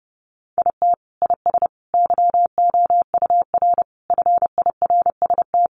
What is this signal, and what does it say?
Morse code, heard as a keyed tone, about 30 wpm, 700 Hz. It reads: IT IS YOUR FIRST
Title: Sentences from Top 1100 Words 30wpm